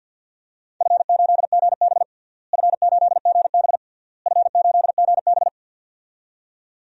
Morse code, heard as a keyed tone, about 52 wpm, 700 Hz. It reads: F8GB F8GB F8GB
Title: DXCC-100 Call Signs - 4 Characters - Repeated 3x - 50wpm